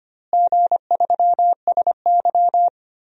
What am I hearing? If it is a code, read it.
Morse code, heard as a keyed tone, about 25 wpm, 700 Hz. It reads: G3SY
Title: DXCC-100 Call Signs - 4 Characters - Repeated 2x - 25wpm